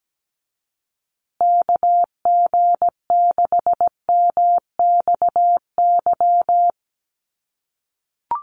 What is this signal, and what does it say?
Morse code, heard as a keyed tone, about 17 wpm, 700 Hz. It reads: KG6MXY
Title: USA Call Signs Repeated 3x 17wpm